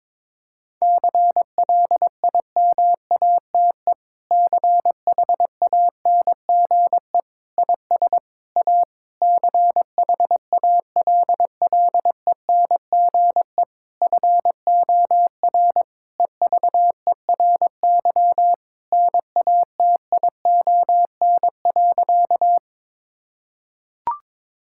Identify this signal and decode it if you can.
Morse code, heard as a keyed tone, about 22 wpm, 700 Hz. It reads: CLIMATE CHANGE IS A CHALLENGE FOR EVERY NATION.
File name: Difficult Sentences with Hints - 22wpm